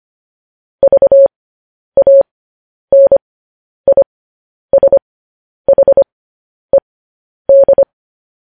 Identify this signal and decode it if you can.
Morse code, heard as a keyed tone, about 25 wpm, spaced as for 10 wpm, 550 Hz. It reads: VANISHED
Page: news-headlines-in-morse-code-at-10-wpm